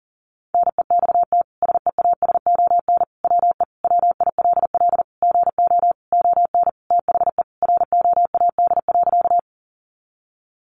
Morse code, heard as a keyed tone, about 40 wpm, 700 Hz. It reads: NEXT SEASON WE WILL GO ON THE ROAD.
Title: Sentences from Top 1400 Words 40wpm